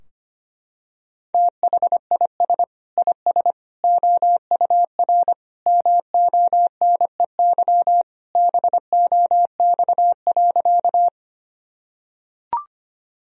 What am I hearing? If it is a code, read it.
Morse code, heard as a keyed tone, about 25 wpm, 700 Hz. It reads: THIS IS OUR MONEY BOX.
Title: Sentences from Top 1000 Words - Review Part 2 - 25wpm